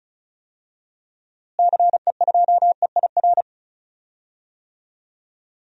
Morse code, heard as a keyed tone, about 36 wpm, 700 Hz. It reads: CE2EIR
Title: DXCC-100 Call Signs - 6 Characters - Repeated 3x - 35wpm